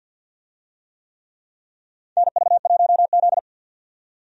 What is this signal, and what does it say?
Morse code, heard as a keyed tone, about 51 wpm, 700 Hz. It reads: NV1Z
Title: USA Call Signs - 4 Characters - 50wpm